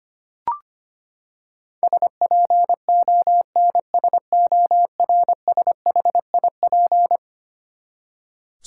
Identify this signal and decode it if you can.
Morse code, heard as a keyed tone, about 25 wpm, 700 Hz. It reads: SPONSORSHIP
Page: morse-code-ninja-practice-25wpm